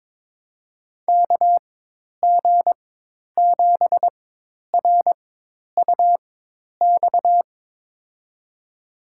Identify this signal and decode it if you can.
Morse code, heard as a keyed tone, about 23 wpm, spaced as for 10 wpm, 700 Hz. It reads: KG7RUX
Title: USA Call Signs - 6 Characters - 4x Character Spacing - 22wpm